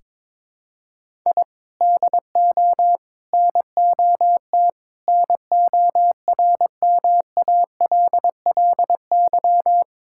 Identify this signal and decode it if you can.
Morse code, heard as a keyed tone, about 22 wpm, 700 Hz. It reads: I DO NOT NORMALLY
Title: Sentences from Top 1000 Words - Review Part 6 - 22wpm